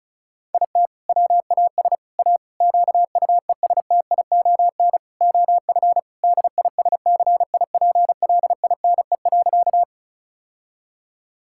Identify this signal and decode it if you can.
Morse code, heard as a keyed tone, about 36 wpm, 700 Hz. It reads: IT WAS A QUESTION OF DISCIPLINE.
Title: Sentences from Top 1000 Words - Review Part 6 - 35wpm